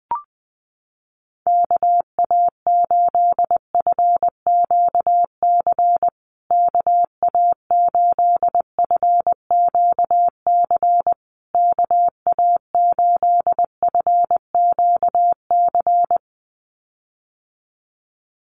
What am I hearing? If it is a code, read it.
Morse code, heard as a keyed tone, about 20 wpm, 700 Hz. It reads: KA8FQC KA8FQC KA8FQC
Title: USA Call Signs Repeated 3x 20wpm